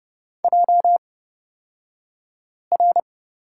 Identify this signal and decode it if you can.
Morse code, heard as a keyed tone, about 29 wpm, spaced as for 5 wpm, 700 Hz. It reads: JR
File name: Lesson 36d - BT - Two Characters 30wpm